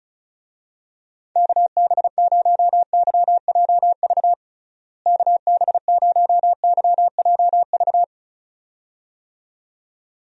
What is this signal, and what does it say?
Morse code, heard as a keyed tone, about 36 wpm, 700 Hz. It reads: KB0YJV KB0YJV
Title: USA Call Signs - 6 Characters - Repeated 2x - 35wpm